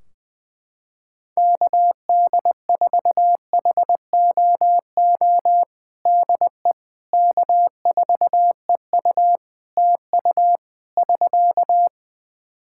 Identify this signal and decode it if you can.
Morse code, heard as a keyed tone, about 20 wpm, 700 Hz. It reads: KD4HOO DE K4EU TU <SK>
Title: Practice QSO - Exchange 4 - Ending - 20wpm